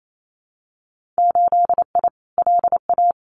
Morse code, heard as a keyed tone, about 28 wpm, 700 Hz. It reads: 8I LA
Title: Winter Field Day Exchanges Encore 28wpm